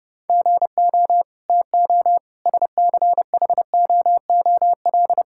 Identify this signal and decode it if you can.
Morse code, heard as a keyed tone, about 30 wpm, 700 Hz. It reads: GO TO SCHOOL
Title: Sentences from Top 1000 Words - Review Part 6 - 30wpm